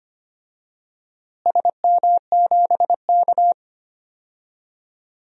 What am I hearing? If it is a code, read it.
Morse code, heard as a keyed tone, about 25 wpm, 700 Hz. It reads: SM7K
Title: DXCC-100 Call Signs - 4 Characters - Repeated 3x - 25wpm